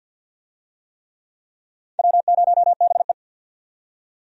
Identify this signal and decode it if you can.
Morse code, heard as a keyed tone, about 48 wpm, 700 Hz. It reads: W0BE
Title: USA Call Signs - 4 Characters - 50wpm